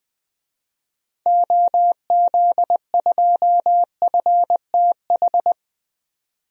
Morse code, heard as a keyed tone, about 20 wpm, 700 Hz. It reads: OZ2FTH
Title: DXCC-100 Call Signs - 6 Characters - 20wpm